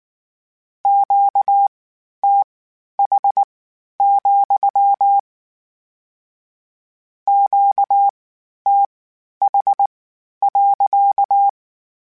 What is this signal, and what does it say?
Morse code, heard as a keyed tone, about 19 wpm, spaced as for 11 wpm, 800 Hz. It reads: QTH, QTH.